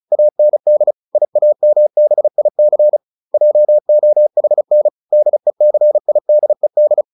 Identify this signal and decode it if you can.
Morse code, heard as a keyed tone, about 34 wpm, 600 Hz. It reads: AND IAMBIC JOHN DECIDED